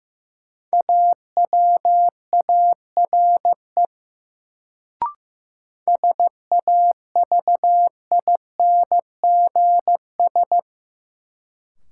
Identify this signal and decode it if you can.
Morse code, heard as a keyed tone, about 15 wpm, 700 Hz. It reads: AWARE SAVINGS
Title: Sentences from Top 1200 Words - New Word Drill Alt - 15wpm